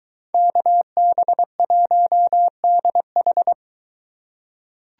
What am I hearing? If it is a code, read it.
Morse code, heard as a keyed tone, about 22 wpm, 700 Hz. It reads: KB1DH